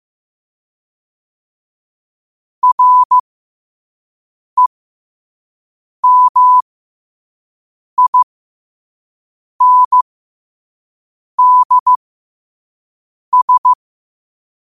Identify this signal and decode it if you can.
Morse code, heard as a keyed tone, about 15 wpm, spaced as for 5 wpm, 1000 Hz. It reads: REMINDS
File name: qotdfiles / 5wpm